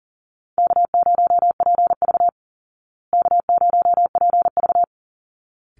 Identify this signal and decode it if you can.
Morse code, heard as a keyed tone, about 40 wpm, 700 Hz. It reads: K0PV K0PV
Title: Call Signs Repeated 3x - 40wpm